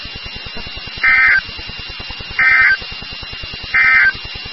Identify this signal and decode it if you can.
Emergency Alert System (SAME) tones, heard as harsh buzzing bursts. It is heard from 0.9 to 4.2 s.
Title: NOAA Weather Radio WXK36